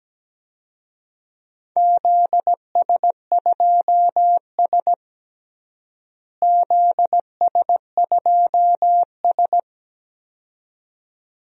Morse code, heard as a keyed tone, about 17 wpm, 700 Hz. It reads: ZS2S ZS2S
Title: DXCC-100 Call Signs - 4 Characters - Repeated 2x - 17wpm